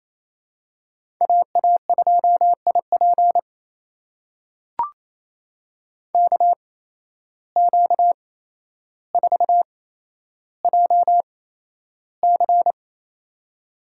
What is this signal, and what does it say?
Morse code, heard as a keyed tone, about 28 wpm, spaced as for 12 wpm, 700 Hz. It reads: AA2IP KQ4JC
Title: USA Call Signs - 5 Characters - 8x Character Spacing - 28wpm